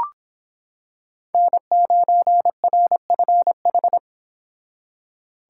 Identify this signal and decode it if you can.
Morse code, heard as a keyed tone, about 26 wpm, 700 Hz. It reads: N9RFH